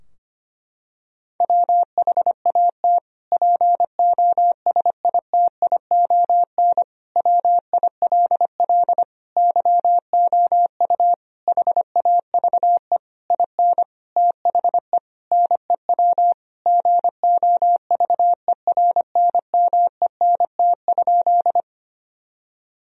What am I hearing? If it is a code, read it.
Morse code, heard as a keyed tone, about 25 wpm, 700 Hz. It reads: WHAT POSITION WILL YOU HAVE IN THE NEW GOVERNMENT?
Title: Sentences from Top 1000 Words - Review Part 3 - 25wpm